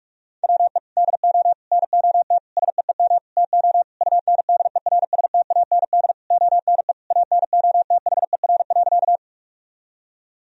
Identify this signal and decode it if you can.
Morse code, heard as a keyed tone, about 44 wpm, 700 Hz. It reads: WE DO NOT SEEM TO UNDERSTAND ONE ANOTHER.
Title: Sentences from Top 1000 Words - Review Part 1 - 45wpm